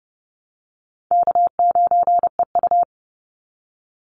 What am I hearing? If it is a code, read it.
Morse code, heard as a keyed tone, about 30 wpm, 700 Hz. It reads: K9EU
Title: Call Signs Repeated 3x - 30wpm